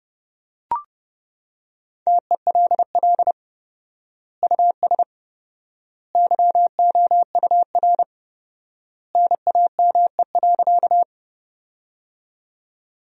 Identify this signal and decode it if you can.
Morse code, heard as a keyed tone, about 30 wpm, 700 Hz. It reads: TELL US YOUR NAME.
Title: 4x Word-Spacing - Sentences from Top 200 Words - 30wpm